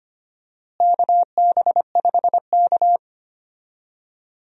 Morse code, heard as a keyed tone, about 25 wpm, 700 Hz. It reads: KB5K
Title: USA Call Signs - 4 Characters - Repeated 3x - 25wpm